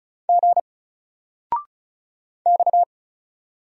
Morse code, heard as a keyed tone, about 36 wpm, 700 Hz. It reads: G X
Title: Single Character (Letter:Number:Punctuation:Prosign) 35wpm